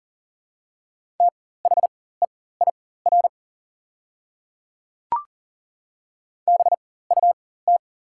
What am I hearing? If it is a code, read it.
Morse code, heard as a keyed tone, about 40 wpm, spaced as for 19 wpm, 690 Hz. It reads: THEIR BUT
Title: 4x Character Spacing - Top 100 Words - 40wpm